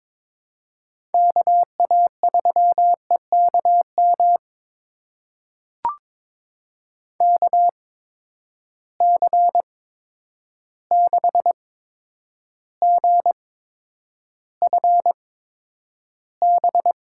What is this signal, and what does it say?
Morse code, heard as a keyed tone, about 22 wpm, spaced as for 9 wpm, 700 Hz. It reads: KA3EKM KC6GFB
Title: USA Call Signs - 6 Characters - 8x Character Spacing - 22wpm